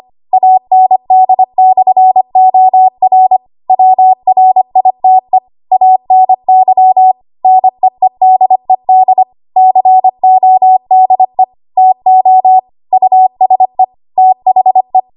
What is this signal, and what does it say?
Morse code, heard as a keyed tone, about 25 wpm, 750 Hz. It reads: AND/OR WRITE ANY NEEDED CODE TO USE THE
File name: Archive / 25 WPM / 25WPM